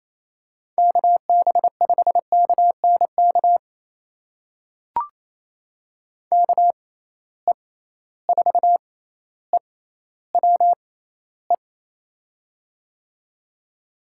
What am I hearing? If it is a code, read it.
Morse code, heard as a keyed tone, about 28 wpm, spaced as for 14 wpm, 700 Hz. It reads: KB5KNK KE4EWE
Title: USA Call Signs - 6 Characters - 6x Character Spacing - 28wpm